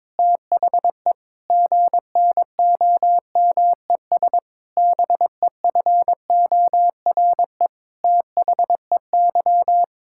Morse code, heard as a keyed tone, about 22 wpm, 700 Hz. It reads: THE GNOMES BEFORE THEY
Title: Pangram Challenge v2 - 22wpm